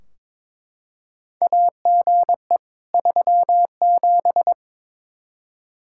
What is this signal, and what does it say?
Morse code, heard as a keyed tone, about 21 wpm, 700 Hz. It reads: AGE 37
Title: Practice QSO - Exchange 3 - More Info - 22wpm